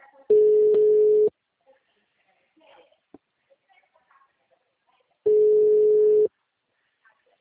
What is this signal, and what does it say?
Ringback tone (Europe, 425 Hz).